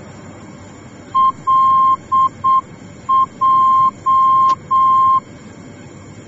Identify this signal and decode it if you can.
Morse code, heard as a keyed tone, about 7 wpm, 1060 Hz. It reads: LJ
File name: germany / mx / LJ-D